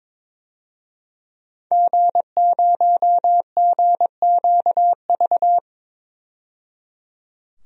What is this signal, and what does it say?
Morse code, heard as a keyed tone, about 22 wpm, 700 Hz. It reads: G0GQV
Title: DXCC-100 Call Signs - 5 Characters - 22wpm